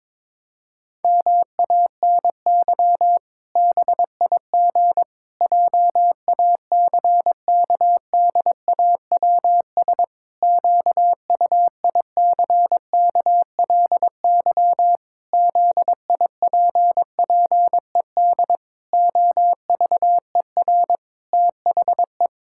Morse code, heard as a keyed tone, about 22 wpm, 700 Hz. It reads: MANY BIG JACKDAWS QUICKLY ZIPPED OVER THE
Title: Pangram Challenge v2 - 22wpm